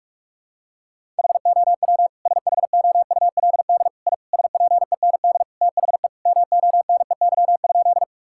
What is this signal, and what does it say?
Morse code, heard as a keyed tone, about 41 wpm, 700 Hz. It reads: HOW SHOULD I SPEND THE MONEY?
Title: Sentences from Top 1000 Words - Review Part 5 - 45wpm